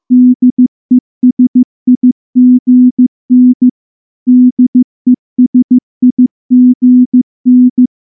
Morse code, heard as a keyed tone, about 14 wpm, 250 Hz. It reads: DESIGN DESIGN
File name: morse / morse-training-15WPM